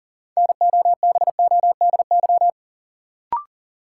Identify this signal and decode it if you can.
Morse code, heard as a keyed tone, about 40 wpm, 700 Hz. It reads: NOBODY
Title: Sentences from Top 1600 Words - New Word Drill Alt - 40wpm